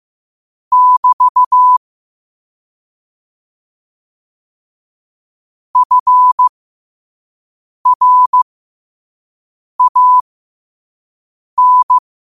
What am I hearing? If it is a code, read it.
Morse code, heard as a keyed tone, about 15 wpm, spaced as for 5 wpm, 1000 Hz. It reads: <BT> FRAN
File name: qotdfiles / 5wpm